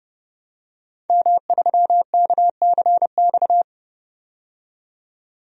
Morse code, heard as a keyed tone, about 31 wpm, 700 Hz. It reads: M3KCX
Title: DXCC-100 Call Signs - 5 Characters - 30wpm